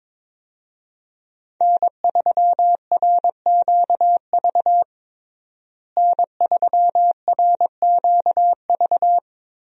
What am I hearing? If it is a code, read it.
Morse code, heard as a keyed tone, about 21 wpm, 700 Hz. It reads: N3RQV N3RQV
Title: USA Call Signs - 5 Characters - Repeated 3x - 22wpm